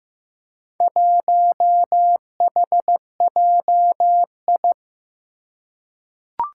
Morse code, heard as a keyed tone, about 15 wpm, 700 Hz. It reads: 1HJI
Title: Four Challenging Letter-Number Combinations 15wpm